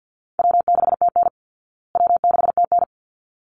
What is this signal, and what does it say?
Morse code, heard as a keyed tone, about 48 wpm, 700 Hz. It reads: W6TN W6TN
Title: Easy Call Signs Repeated 2x - 50wpm